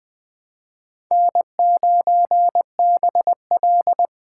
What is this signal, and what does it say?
Morse code, heard as a keyed tone, about 21 wpm, 700 Hz. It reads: N9BL
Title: Speed Racing - USA 1x2 Call Signs 17wpm